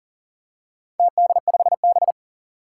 Morse code, heard as a keyed tone, about 40 wpm, 700 Hz. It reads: TD5B
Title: Four Challenging Letter-Number Combinations 40wpm